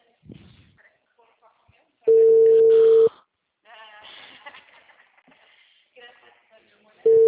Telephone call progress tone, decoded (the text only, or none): ringback tone